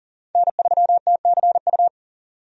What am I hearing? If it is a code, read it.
Morse code, heard as a keyed tone, about 40 wpm, 700 Hz. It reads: N3TCU
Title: USA Call Signs - 5 Characters - Repeated 2x - 40wpm